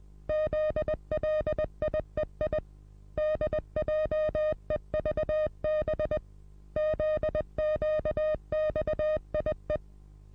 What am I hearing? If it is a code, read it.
Morse code, heard as a keyed tone, about 20 wpm, 620 Hz. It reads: ZLIEI DJEVB ZQXIE
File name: repos / Morsekurs / DJ4UF Morsen27